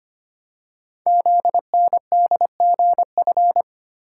Morse code, heard as a keyed tone, about 25 wpm, 700 Hz. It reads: ZNDGF
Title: Code Groups - 5 Letters - 25wpm